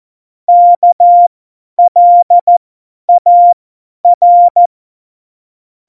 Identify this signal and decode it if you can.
Morse code, heard as a keyed tone, about 14 wpm, spaced as for 10 wpm, 700 Hz. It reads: KLAR
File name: Morsekurs / Witze / Witz 05. Dezember 2017, 50 F70